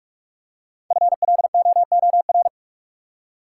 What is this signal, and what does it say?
Morse code, heard as a keyed tone, about 45 wpm, 700 Hz. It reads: FLOOR
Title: Five Letter Words - 45wpm